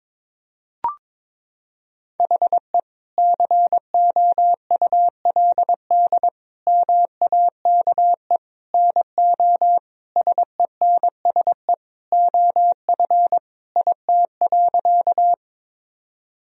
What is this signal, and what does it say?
Morse code, heard as a keyed tone, about 22 wpm, 700 Hz. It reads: HE COULD MAKE NO SENSE OF IT.
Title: Sentences from Top 1000 Words - Review Part 4 - 22wpm